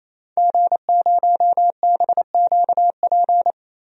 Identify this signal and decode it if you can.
Morse code, heard as a keyed tone, about 28 wpm, 700 Hz. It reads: G0BQP
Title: DXCC-100 Call Signs - 5 Characters - Repeated 3x - 28wpm